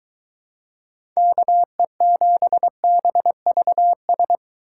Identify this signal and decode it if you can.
Morse code, heard as a keyed tone, about 22 wpm, 700 Hz. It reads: KE7BVS